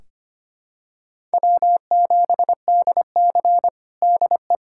Morse code, heard as a keyed tone, about 25 wpm, 700 Hz. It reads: W7DC DE
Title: USA Callsign DE Callsign - 4 Characters - 25wpm